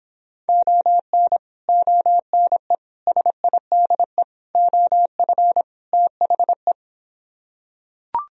Morse code, heard as a keyed tone, about 26 wpm, 700 Hz. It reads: ON ONE SIDE OF THE